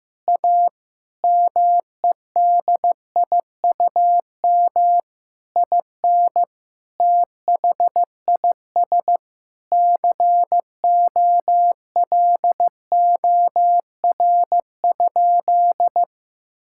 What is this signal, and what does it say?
Morse code, heard as a keyed tone, about 15 wpm, 700 Hz. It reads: A MEDIUM IN THIS COLOR?
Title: Sentences from Top 1000 Words - Review Part 6 - 15wpm